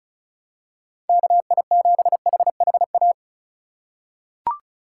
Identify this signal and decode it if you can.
Morse code, heard as a keyed tone, about 36 wpm, 700 Hz. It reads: KI7HHA
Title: USA Call Signs - 2x Character Spacing - 35wpm